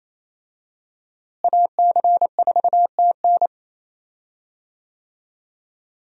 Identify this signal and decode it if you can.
Morse code, heard as a keyed tone, about 28 wpm, 700 Hz. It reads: AC4TN
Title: USA Call Signs - 5 Characters - Repeated 3x - 28wpm